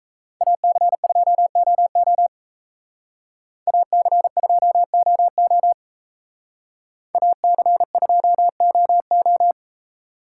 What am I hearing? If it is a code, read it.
Morse code, heard as a keyed tone, about 33 wpm, 700 Hz. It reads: AC2OO AC2OO AC2OO
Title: Speed Racing - USA 2x2 Call Signs 28wpm